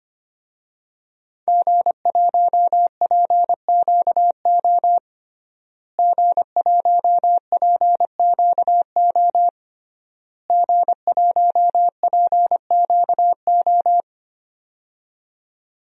Morse code, heard as a keyed tone, about 25 wpm, 700 Hz. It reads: G1PQO G1PQO G1PQO
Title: DXCC-100 Call Signs - 5 Characters - Repeated 3x - 25wpm